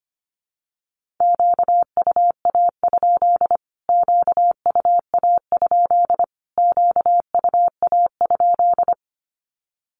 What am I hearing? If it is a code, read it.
Morse code, heard as a keyed tone, about 25 wpm, 700 Hz. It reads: QUA? QUA? QUA?
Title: Q-Codes Repeated 3x 25wpm